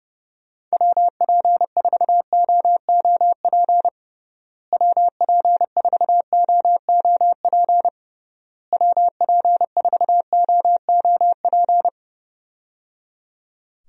Morse code, heard as a keyed tone, about 30 wpm, 700 Hz. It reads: WP4OOP WP4OOP WP4OOP
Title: USA Call Signs - 6 Characters - Repeated 3x - 30wpm